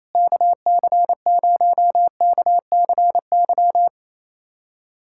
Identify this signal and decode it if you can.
Morse code, heard as a keyed tone, about 27 wpm, 700 Hz. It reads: KC0KCY
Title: USA Callsign DE Callsign 28wpm